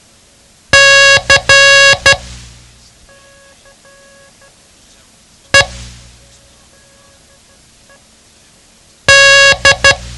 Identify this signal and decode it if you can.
Morse code, heard as a keyed tone, about 12 wpm, 1650 Hz. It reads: C E D
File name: ham radio / CodeQuick / Code Quick 07